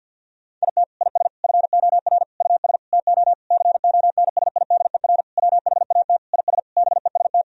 Morse code, heard as a keyed tone, about 49 wpm, 700 Hz. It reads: IT IS FOR US TO CONSIDER WHAT IS BEST
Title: Sentences from Top 1000 Words - Review Part 4 - 50wpm